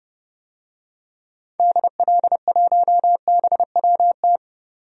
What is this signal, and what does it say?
Morse code, heard as a keyed tone, about 30 wpm, 700 Hz. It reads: DL1BWT
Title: DXCC-100 Call Signs - 6 Characters - 30wpm